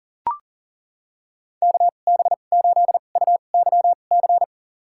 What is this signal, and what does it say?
Morse code, heard as a keyed tone, about 37 wpm, spaced as for 28 wpm, 700 Hz. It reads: KB8UYC
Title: USA Call Signs - 2x Character Spacing - 40wpm